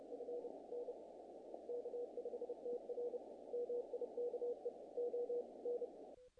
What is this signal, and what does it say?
Morse code, heard as a keyed tone, about 33 wpm, 500 Hz, with noise. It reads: F N K5TR MIKE O N
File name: contests / naqp-cw-k5tr-op-k5pi- -1800